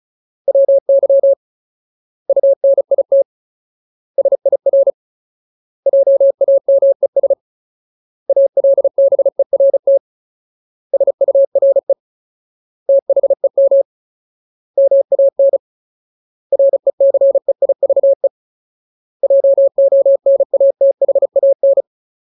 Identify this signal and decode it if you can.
Morse code, heard as a keyed tone, about 34 wpm, 550 Hz. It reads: WY UNIT SIR JAMES ALBERT SURE THEM MAN RECEIVE JONATHAN
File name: Ultimate Word List / 35WPM Ultimate